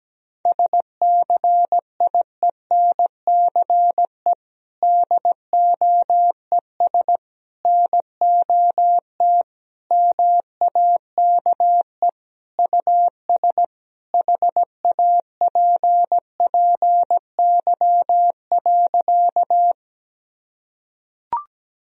Morse code, heard as a keyed tone, about 17 wpm, 700 Hz. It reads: SCIENCE DOES NOT MAKE US HAPPY.
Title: Sentences from Top 1000 Words - Review Part 2 - 17wpm